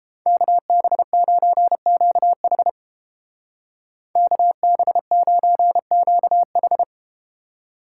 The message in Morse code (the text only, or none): KB9QH KB9QH